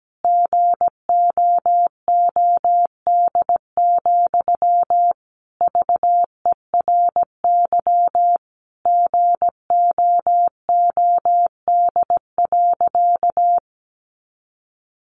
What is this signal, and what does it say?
Morse code, heard as a keyed tone, about 17 wpm, 700 Hz. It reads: GOOD, VERY GOOD.
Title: Sentences from Top 200 Words 17wpm